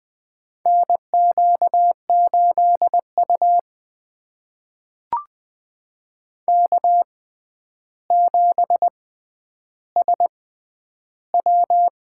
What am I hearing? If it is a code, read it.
Morse code, heard as a keyed tone, about 20 wpm, spaced as for 10 wpm, 700 Hz. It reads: NQ8U K7SW
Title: USA Call Signs - 4 Characters - 6x Character Spacing - 20wpm